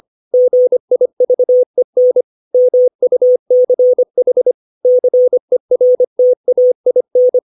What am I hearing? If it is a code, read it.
Morse code, heard as a keyed tone, about 25 wpm, 500 Hz. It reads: GIVEN MUCH CERTAIN
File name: 632 words / 632 Words 25WPM